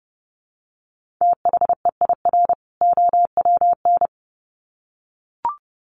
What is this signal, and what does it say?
Morse code, heard as a keyed tone, about 30 wpm, 700 Hz. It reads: THEIR OWN
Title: Sets of 2 Words 30wpm